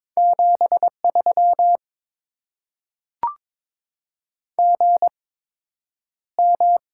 Morse code, heard as a keyed tone, about 22 wpm, spaced as for 9 wpm, 700 Hz. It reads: 73 GM
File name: Top 20 QSO Elements - 8x Character Spacing - 22wpm